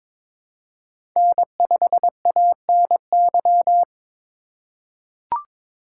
Morse code, heard as a keyed tone, about 23 wpm, 700 Hz. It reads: N5ANY
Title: USA Call Signs - 5 Characters - 2x Character Spacing - 22wpm